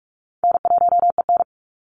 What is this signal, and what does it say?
Morse code, heard as a keyed tone, about 44 wpm, 700 Hz. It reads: N1EN
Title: Call Signs Repeated 3x - 45wpm